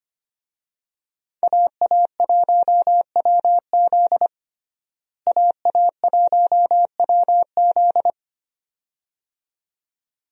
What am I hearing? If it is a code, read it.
Morse code, heard as a keyed tone, about 25 wpm, 700 Hz. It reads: AA1WZ AA1WZ
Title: USA Call Signs - 5 Characters - Repeated 2x - 25wpm